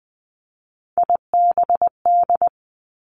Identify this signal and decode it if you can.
Morse code, heard as a keyed tone, about 20 wpm, 700 Hz. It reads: IBD
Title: Three Challenging Letter-Number Combinations 20wpm